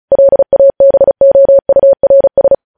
Morse code, heard as a keyed tone, about 36 wpm, 550 Hz. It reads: LABOURS
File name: podcast / News headlines-1x-35x35